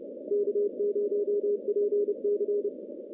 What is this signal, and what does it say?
Morse code, heard as a keyed tone, about 28 wpm, 410 Hz. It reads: K0PC
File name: contests / CW-SS-k5tr- -1400